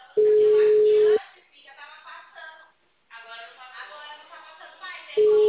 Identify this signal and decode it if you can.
Ringback tone (Europe, 425 Hz), with speech.